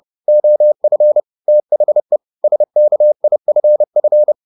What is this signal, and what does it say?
Morse code, heard as a keyed tone, about 30 wpm, 600 Hz. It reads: OF THE SKIFF